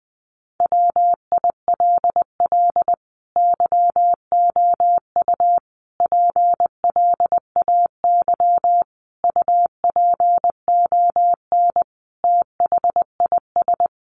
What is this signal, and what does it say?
Morse code, heard as a keyed tone, about 19 wpm, 700 Hz. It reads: WILL YOU PLAY UPON THIS
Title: Sentences from Top 1500 Words 20wpm